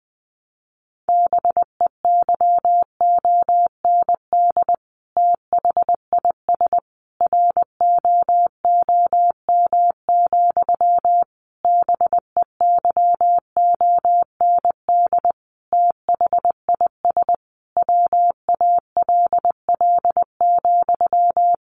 Morse code, heard as a keyed tone, about 20 wpm, 700 Hz. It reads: BEYOND THIS ROOM, BEYOND THIS WALL,
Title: A Sound Of Thunder 20wpm